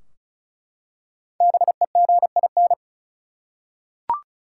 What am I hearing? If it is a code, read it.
Morse code, heard as a keyed tone, about 36 wpm, 700 Hz. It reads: BEGIN
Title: Five Letter Words - 35wpm